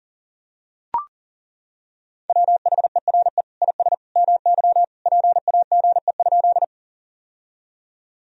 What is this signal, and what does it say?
Morse code, heard as a keyed tone, about 40 wpm, 700 Hz. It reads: WHERE IS MY PAGE?
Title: Sentences from Top 1000 Words - Review Part 1 - 40wpm